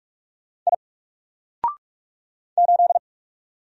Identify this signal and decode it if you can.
Morse code, heard as a keyed tone, about 44 wpm, 700 Hz. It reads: I 8
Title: Single Letter-Number Alt 45wpm